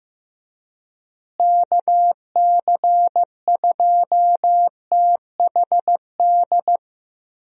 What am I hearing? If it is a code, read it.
Morse code, heard as a keyed tone, about 16 wpm, 700 Hz. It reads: KC2THD
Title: USA Call Signs - 6 Characters - Repeated 3x - 15wpm